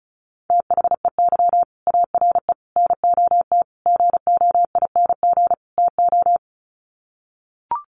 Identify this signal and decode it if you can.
Morse code, heard as a keyed tone, about 34 wpm, 700 Hz. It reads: THEY ARE NOT GOING TO
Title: Sets of 5 Words Encore2 35wpm